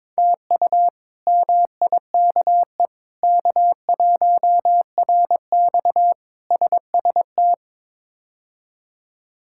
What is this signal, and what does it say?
Morse code, heard as a keyed tone, about 22 wpm, 700 Hz. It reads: TU MIKE K1RX SST
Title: K1USN SST QSO Practice - 22wpm